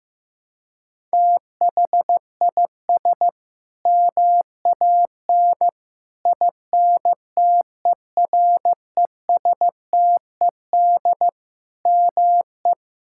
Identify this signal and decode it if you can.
Morse code, heard as a keyed tone, about 15 wpm, 700 Hz. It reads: THIS MAN INTERESTED ME
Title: Sentences from Top 1000 Words - Review Part 6 - 15wpm